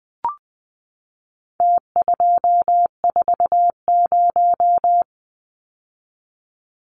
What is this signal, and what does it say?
Morse code, heard as a keyed tone, about 20 wpm, 700 Hz. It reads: T240
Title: Numbers - Contest Style - 4-Digit 20wpm